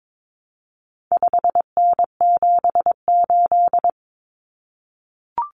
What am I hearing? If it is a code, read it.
Morse code, heard as a keyed tone, about 22 wpm, 700 Hz. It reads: 5N78
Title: Common Cut Numbers - 4-Digit 22wpm